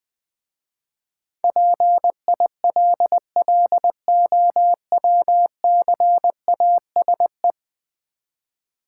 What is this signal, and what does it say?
Morse code, heard as a keyed tone, about 20 wpm, 700 Hz. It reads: PILLOWCASE
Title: Easy Words with Hints - 20wpm